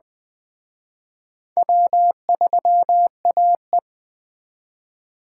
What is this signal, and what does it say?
Morse code, heard as a keyed tone, about 20 wpm, 700 Hz. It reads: W3AE
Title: Speed Racing - USA 1x2 Call Signs 17wpm